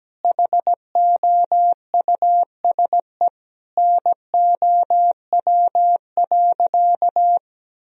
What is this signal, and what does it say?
Morse code, heard as a keyed tone, about 17 wpm, 700 Hz. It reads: HOUSE NOW.
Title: Sentences from Top 1000 Words - Review Part 2 - 17wpm